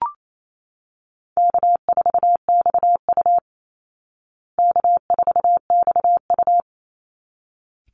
Morse code, heard as a keyed tone, about 28 wpm, 700 Hz. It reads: K4XU K4XU
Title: Easy Call Signs Repeated 2x - 28wpm